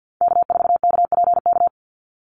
Morse code, heard as a keyed tone, about 51 wpm, 700 Hz. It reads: K4KPK
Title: NA SOTA Activator Callsigns 50wpm